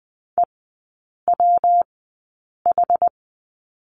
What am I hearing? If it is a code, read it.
Morse code, heard as a keyed tone, about 20 wpm, 700 Hz. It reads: E W H
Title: 114 Lesson 19e - Letter w - Three Characters 20wpm